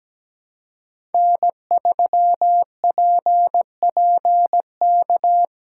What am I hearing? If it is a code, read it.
Morse code, heard as a keyed tone, about 17 wpm, 700 Hz. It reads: N3PPK
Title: USA Callsign DE Callsign - 5 Characters - 17wpm